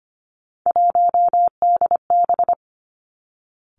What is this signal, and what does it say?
Morse code, heard as a keyed tone, about 25 wpm, 700 Hz. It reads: 1DB
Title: Three Challenging Letter-Number Combinations 25wpm